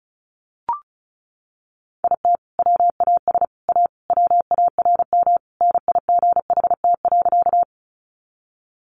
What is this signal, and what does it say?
Morse code, heard as a keyed tone, about 34 wpm, 700 Hz. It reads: IT WAS A WARM NIGHT.
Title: Sentences from Top 500 Words 35wpm